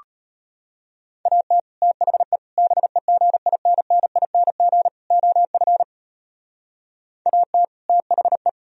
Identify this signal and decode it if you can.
Morse code, heard as a keyed tone, about 33 wpm, 700 Hz. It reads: AT THE BEGINNING OF AT THE